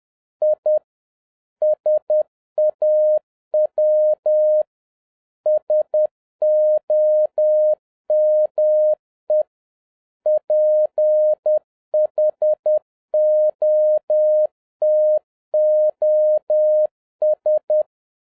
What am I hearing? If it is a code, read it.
Morse code, heard as a keyed tone, about 10 wpm, 610 Hz. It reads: I SAW SOME PHOTOS
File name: 2011-07-mp3 / n31 THE 3.11 2 10wpm